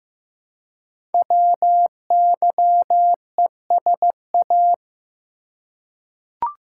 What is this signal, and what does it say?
Morse code, heard as a keyed tone, about 15 wpm, 700 Hz. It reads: WYESA
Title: Code Groups - 5 Letters - 15wpm